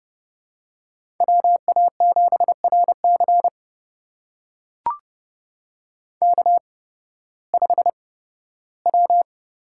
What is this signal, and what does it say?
Morse code, heard as a keyed tone, about 31 wpm, spaced as for 16 wpm, 700 Hz. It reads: WA7RC K5W
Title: USA Call Signs - 5 Characters - 8x Character Spacing - 30wpm